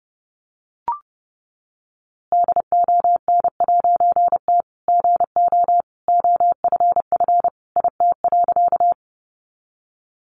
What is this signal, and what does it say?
Morse code, heard as a keyed tone, about 30 wpm, 700 Hz. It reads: DON'T GO OFF IT.
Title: A Sound Of Thunder 30wpm